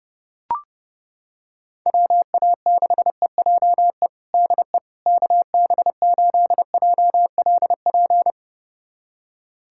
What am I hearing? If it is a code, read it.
Morse code, heard as a keyed tone, about 32 wpm, 700 Hz. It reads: WA6EJE DE KB8JLP
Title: USA Callsign DE Callsign - 6 Characters - 30wpm